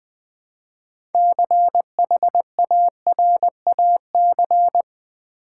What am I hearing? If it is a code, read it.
Morse code, heard as a keyed tone, about 20 wpm, 700 Hz. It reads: CHARAC
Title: Build A Word - 3 Letters At A Time - 9-Letter Words - 20wpm